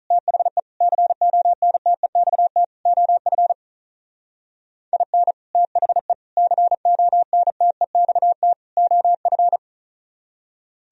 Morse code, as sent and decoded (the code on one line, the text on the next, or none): - .... / -.-. --- -. - . -..- - / --- ..-. / .. -. / - .... / -.-. --- -. - . -..- - / --- ..-.
THE CONTEXT OF IN THE CONTEXT OF